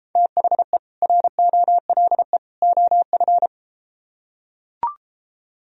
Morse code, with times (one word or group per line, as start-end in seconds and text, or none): 0.2-0.8 THE
1.0-2.4 ROLE
2.6-3.5 OF